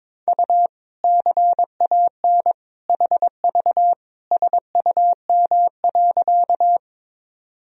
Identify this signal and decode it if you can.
Morse code, heard as a keyed tone, about 22 wpm, 700 Hz. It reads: U CAN HV SUM.
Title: Sentences from Top 100 Words and Abbreviations 22wpm